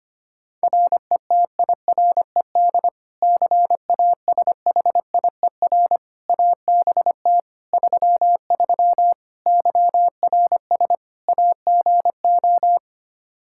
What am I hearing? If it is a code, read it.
Morse code, heard as a keyed tone, about 25 wpm, 700 Hz. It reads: RETIRED CASHIER ABT 33 YRS AGO
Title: Practice QSO - Exchange 3 - More Info - 25wpm